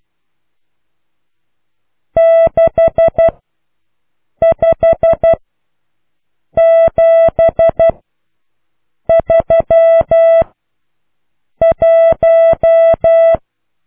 Morse code, heard as a keyed tone, about 12 wpm, spaced as for 6 wpm, 660 Hz. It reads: 65731